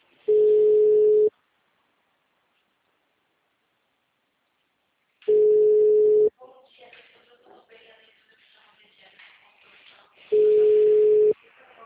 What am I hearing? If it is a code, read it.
Ringback tone (Europe, 425 Hz), with speech.